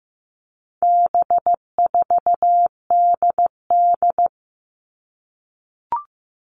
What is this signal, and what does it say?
Morse code, heard as a keyed tone, about 15 wpm, 700 Hz. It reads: B4DD
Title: All Cut Numbers - 4-Digit 15wpm